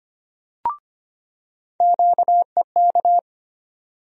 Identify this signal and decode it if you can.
Morse code, heard as a keyed tone, about 25 wpm, 700 Hz. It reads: QEK
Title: USA Call Sign Postfixes - 3 Letters 25wpm